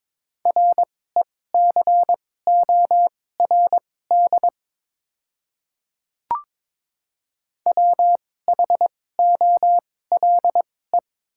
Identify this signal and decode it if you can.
Morse code, heard as a keyed tone, about 22 wpm, spaced as for 16 wpm, 690 Hz. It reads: RECORD WHOLE
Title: 2x Character Spacing - Top 500 Words - 22wpm